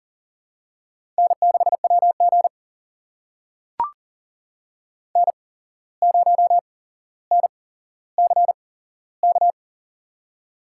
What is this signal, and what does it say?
Morse code, heard as a keyed tone, about 37 wpm, spaced as for 16 wpm, 700 Hz. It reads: N6WG N0NCK
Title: USA Call Signs - 8x Character Spacing - 40wpm